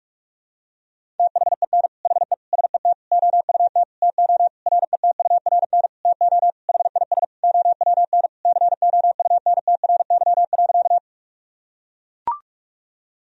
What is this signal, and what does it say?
Morse code, heard as a keyed tone, about 42 wpm, 700 Hz. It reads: THEN HE SET OUT TO RETURN TO HIS OWN COUNTRY.
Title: Sentences from Top 1000 Words - Review Part 1 - 45wpm